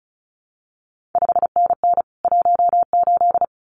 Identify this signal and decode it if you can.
Morse code, heard as a keyed tone, about 36 wpm, 700 Hz. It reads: 5NN 18
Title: CQ WW DX Contest Exchange Encore - 35wpm